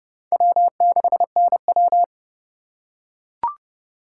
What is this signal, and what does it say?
Morse code, heard as a keyed tone, about 30 wpm, 700 Hz. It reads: W6NW